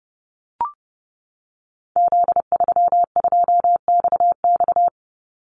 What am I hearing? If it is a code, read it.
Morse code, heard as a keyed tone, about 28 wpm, 700 Hz. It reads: Z32XX
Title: Call Signs Repeated 2x - 30wpm